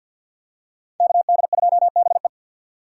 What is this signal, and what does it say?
Morse code, heard as a keyed tone, about 52 wpm, 700 Hz. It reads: KD2BE
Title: USA Call Signs - 5 Characters - 6x Character Spacing - 50wpm